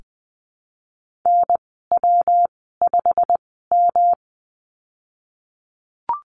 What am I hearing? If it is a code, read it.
Morse code, heard as a keyed tone, about 20 wpm, spaced as for 15 wpm, 700 Hz. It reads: NW5M
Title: 186 Lesson 30g - Letter b - Call Signs 20wpm